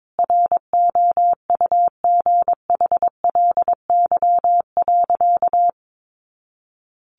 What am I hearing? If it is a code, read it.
Morse code, heard as a keyed tone, about 22 wpm, 700 Hz. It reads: ROUGHLY.
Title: Sentences from Top 1500 Words 22wpm